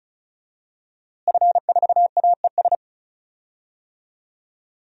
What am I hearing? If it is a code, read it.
Morse code, heard as a keyed tone, about 35 wpm, 700 Hz. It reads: F4AES
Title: DXCC-100 Call Signs - 5 Characters - Repeated 2x - 35wpm